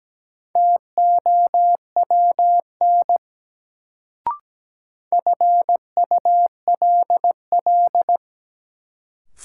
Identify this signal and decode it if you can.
Morse code, heard as a keyed tone, about 17 wpm, 700 Hz. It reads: TOWN FULL
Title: Sentences from Top 500 Words - New Word Drill - 17wpm